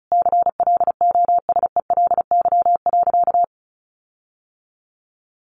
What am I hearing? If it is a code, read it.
Morse code, heard as a keyed tone, about 34 wpm, 700 Hz. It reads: CLOSELY.
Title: Sentences from Top 1400 Words 35wpm